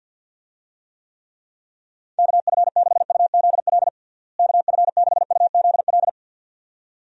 Morse code, heard as a keyed tone, about 51 wpm, 700 Hz. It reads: KF6UZL KF6UZL
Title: USA Call Signs - 6 Characters - Repeated 2x - 50wpm